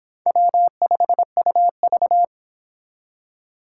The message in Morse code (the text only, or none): W5UV